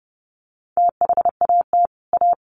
Morse code, heard as a keyed tone, about 30 wpm, 700 Hz. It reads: THAT A
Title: Sets of 4 Words 30wpm